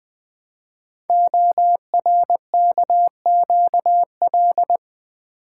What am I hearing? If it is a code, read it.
Morse code, heard as a keyed tone, about 20 wpm, 700 Hz. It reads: ORKQL
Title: Code Groups - 5 Letters - 20wpm